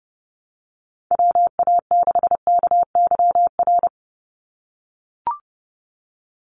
Morse code, heard as a keyed tone, about 30 wpm, 700 Hz. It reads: WA6KYR
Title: NA SOTA Chaser Callsigns 30wpm